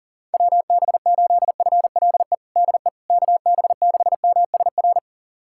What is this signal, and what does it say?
Morse code, heard as a keyed tone, about 40 wpm, 700 Hz. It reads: WB8FLE DE KB6MSR
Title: USA Callsign DE Callsign - 6 Characters - 40wpm